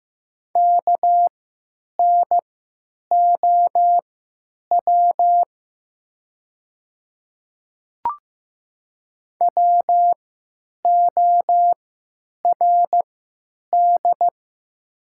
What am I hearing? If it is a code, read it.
Morse code, heard as a keyed tone, about 15 wpm, spaced as for 9 wpm, 700 Hz. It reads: KNOW WORD
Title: 3x Character Spacing - Top 100 Words - 15wpm